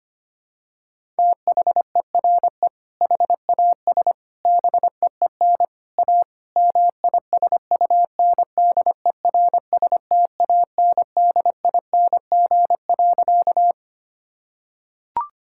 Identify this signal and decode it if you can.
Morse code, heard as a keyed tone, about 25 wpm, 700 Hz. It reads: THERE HAS BEEN A MISUNDERSTANDING.
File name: practice / 25wpm / Sentences with 4-Word Phrases - 25wpm